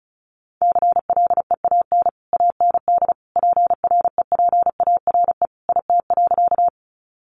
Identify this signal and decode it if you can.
Morse code, heard as a keyed tone, about 34 wpm, 700 Hz. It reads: CLEAN AND PREPARE IT.
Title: Sentences from Top 1100 Words 35wpm